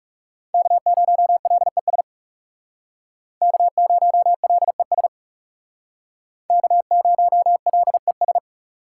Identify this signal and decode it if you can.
Morse code, heard as a keyed tone, about 40 wpm, 700 Hz. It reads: K0LES K0LES K0LES